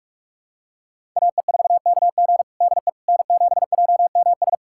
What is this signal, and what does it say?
Morse code, heard as a keyed tone, about 41 wpm, 700 Hz. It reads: AE4KG DE N7JMS
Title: USA Callsign DE Callsign - 5 Characters - 45wpm